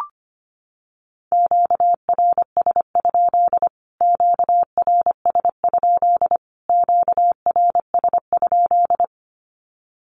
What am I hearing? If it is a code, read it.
Morse code, heard as a keyed tone, about 25 wpm, 700 Hz. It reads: QRS? QRS? QRS?
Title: Q-Codes Repeated 3x 25wpm